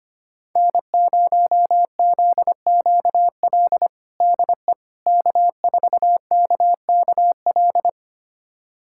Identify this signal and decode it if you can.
Morse code, heard as a keyed tone, about 25 wpm, 700 Hz. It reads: N0ZQL DE K4KKL
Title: USA Callsign DE Callsign - 5 Characters - 25wpm